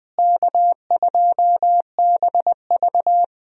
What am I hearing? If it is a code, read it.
Morse code, heard as a keyed tone, about 20 wpm, 700 Hz. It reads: K2BV
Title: USA Call Signs - 4 Characters - Repeated 2x - 20wpm